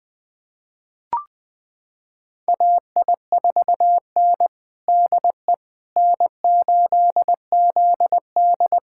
Morse code, heard as a keyed tone, about 20 wpm, 700 Hz. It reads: AI4N DE N8ZD
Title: USA Callsign DE Callsign - 4 Characters - 20wpm